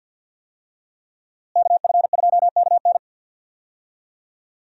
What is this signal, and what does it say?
Morse code, heard as a keyed tone, about 46 wpm, 700 Hz. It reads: KF2KN
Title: USA Call Signs - 5 Characters - Repeated 2x - 50wpm